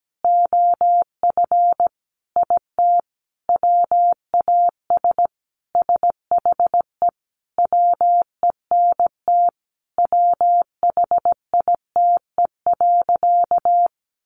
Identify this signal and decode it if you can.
Morse code, heard as a keyed tone, about 17 wpm, 700 Hz. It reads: OF IT WAS SHE WENT WHITE.
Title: Sentences from Top 1200 Words 17wpm